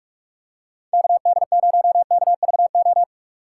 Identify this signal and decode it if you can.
Morse code, heard as a keyed tone, about 41 wpm, 700 Hz. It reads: KD0KVO
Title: USA Call Signs - 6 Characters - 3x Character Spacing - 45wpm